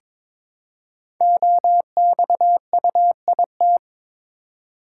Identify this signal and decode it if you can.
Morse code, heard as a keyed tone, about 22 wpm, 700 Hz. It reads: OXUIT
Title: Code Groups - 5 Letters - 22wpm